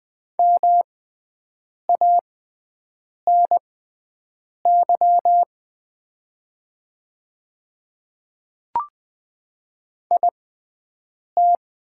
Morse code, heard as a keyed tone, about 20 wpm, spaced as for 7 wpm, 700 Hz. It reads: MANY IT
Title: 6x Character Spacing - Top 300 Words - 20wpm